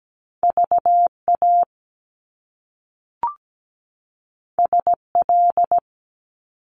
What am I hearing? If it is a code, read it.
Morse code, heard as a keyed tone, about 17 wpm, 700 Hz. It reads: VA SL
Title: Top Two Letter Combinations in Words 17wpm